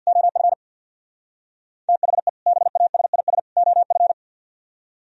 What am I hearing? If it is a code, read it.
Morse code, heard as a keyed tone, about 64 wpm, 700 Hz. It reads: OF THE BASIS OF